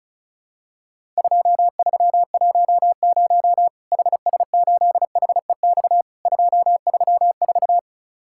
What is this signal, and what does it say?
Morse code, heard as a keyed tone, about 34 wpm, 700 Hz. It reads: 2310 HS8HEX 234
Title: Worked All Europe DX Contest - QTC Exchange - 35wpm